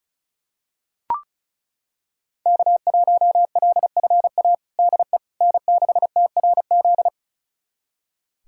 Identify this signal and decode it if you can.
Morse code, heard as a keyed tone, about 36 wpm, 700 Hz. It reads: K1LFA DE N6TRZ
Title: USA Callsign DE Callsign - 5 Characters - 35wpm